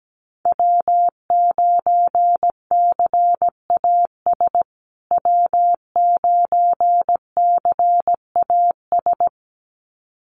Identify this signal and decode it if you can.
Morse code, heard as a keyed tone, about 17 wpm, 700 Hz. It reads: W9CAS W9CAS
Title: USA Call Signs Repeated 2x 17wpm